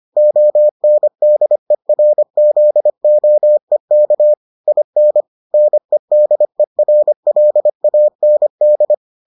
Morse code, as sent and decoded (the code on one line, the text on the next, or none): --- -. -.. .-. --.. --- . -.- / .. -. / -. -.. .-. .-.. .- -. -..
ONDERZOEK IN NEDERLAND